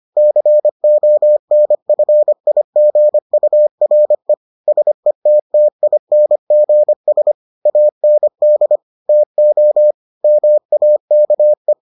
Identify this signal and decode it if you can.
Morse code, heard as a keyed tone, about 25 wpm, 600 Hz. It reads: CONFIGURE SETTINGS AND TO MAKE